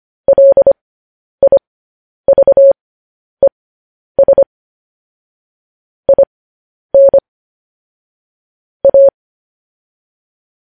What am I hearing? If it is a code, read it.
Morse code, heard as a keyed tone, about 25 wpm, spaced as for 10 wpm, 550 Hz. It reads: LIVES IN A
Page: news headlines in morse code at 10 wpm